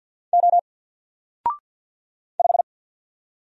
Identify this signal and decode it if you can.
Morse code, heard as a keyed tone, about 48 wpm, 700 Hz. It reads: O 5
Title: Single Character (Letter:Number:Punctuation:Prosign) 50wpm